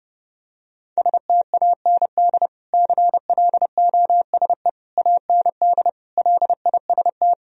Morse code, heard as a keyed tone, about 32 wpm, 700 Hz. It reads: STAND CLOSE AND LIST
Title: Sentences from Top 1000 Words - Review Part 2 - 30wpm